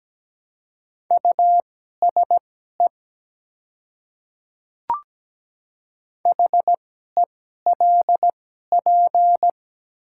Morse code, heard as a keyed tone, about 17 wpm, spaced as for 12 wpm, 700 Hz. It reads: USE HELP
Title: 2x Character Spacing - Top 200 Words - 17wpm